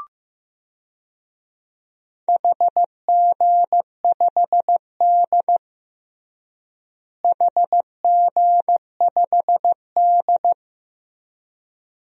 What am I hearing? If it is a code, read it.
Morse code, heard as a keyed tone, about 15 wpm, 700 Hz. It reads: HG5D HG5D
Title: DXCC-100 Call Signs - 4 Characters - Repeated 2x - 15wpm